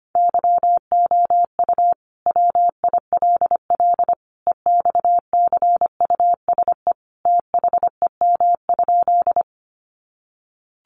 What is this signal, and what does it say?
Morse code, heard as a keyed tone, about 25 wpm, 700 Hz. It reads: YOU WILL EXCUSE THEM?
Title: Sentences from Top 700 Words 25wpm